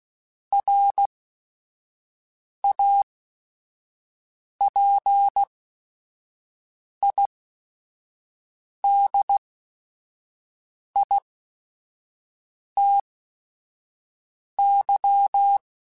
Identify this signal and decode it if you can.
Morse code, heard as a keyed tone, about 16 wpm, spaced as for 5 wpm, 800 Hz. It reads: RAPIDITY
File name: Technical Subjects / Technical Subjects-mp3s / XC @5